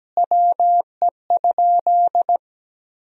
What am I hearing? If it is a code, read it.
Morse code, heard as a keyed tone, about 17 wpm, 700 Hz. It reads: WE?
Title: Sentences from Top 1000 Words - Review Part 3 - 17wpm